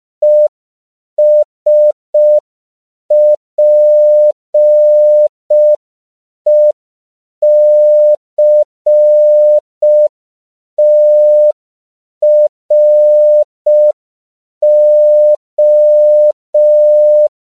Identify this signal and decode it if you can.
Morse code, heard as a keyed tone, about 5 wpm, 600 Hz. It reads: ESPECTRO
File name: CW / 5PPM